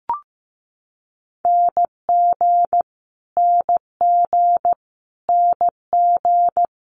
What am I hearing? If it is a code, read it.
Morse code, heard as a keyed tone, about 15 wpm, 700 Hz. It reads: NG NG NG
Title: Multi-letter Phonemes Repeated 3x 15wpm